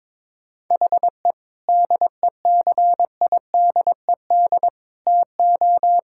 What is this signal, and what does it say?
Morse code, heard as a keyed tone, about 22 wpm, 700 Hz. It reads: HE DECIDED TO
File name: Sentences with 7-Word Phrases - 22wpm